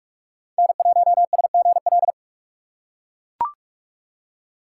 Morse code, heard as a keyed tone, about 42 wpm, 700 Hz. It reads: N1SGL
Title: USA Call Signs - 5 Characters - 2x Character Spacing - 45wpm